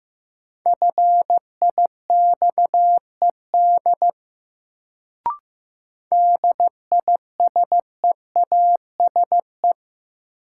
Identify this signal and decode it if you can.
Morse code, heard as a keyed tone, about 15 wpm, 700 Hz. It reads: FIXED DISEASE
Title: Sentences from Top 1100 Words - New Word Drill Alt - 15wpm